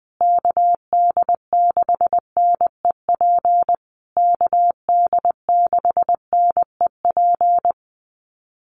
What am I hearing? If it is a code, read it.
Morse code, heard as a keyed tone, about 20 wpm, 700 Hz. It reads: KD6NEP KD6NEP
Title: USA Call Signs Repeated 3x 20wpm